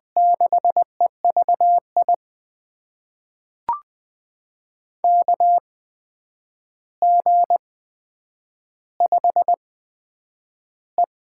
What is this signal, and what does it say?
Morse code, heard as a keyed tone, about 20 wpm, spaced as for 9 wpm, 700 Hz. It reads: 6EVI KG5E
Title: USA Call Signs - 6 Characters - 8x Character Spacing - 20wpm